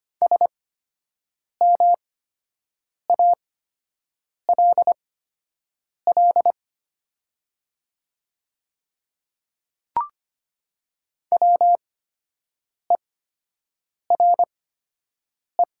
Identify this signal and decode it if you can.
Morse code, heard as a keyed tone, about 25 wpm, spaced as for 7 wpm, 700 Hz. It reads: SMALL WERE
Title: 8x Character Spacing - Top 300 Words - 25wpm